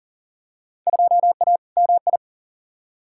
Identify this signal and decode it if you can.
Morse code, heard as a keyed tone, about 40 wpm, 700 Hz. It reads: 2A MI
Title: Field Day Exchanges - 40wpm